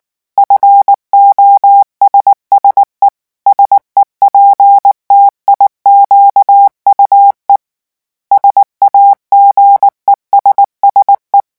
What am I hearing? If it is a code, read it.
Morse code, heard as a keyed tone, about 19 wpm, 800 Hz. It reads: FOSSE SEPTIQUE SAGESSE
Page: REVISION 8 - 8 mots minute